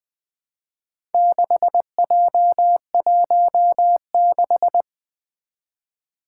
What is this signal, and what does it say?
Morse code, heard as a keyed tone, about 20 wpm, 700 Hz. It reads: 6J16
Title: Four Challenging Letter-Number Combinations 20wpm